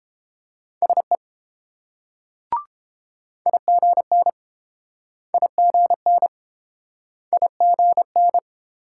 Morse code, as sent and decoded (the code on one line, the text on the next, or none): ... / .. --. -. / .. --. -. / .. --. -.
SE IGN IGN IGN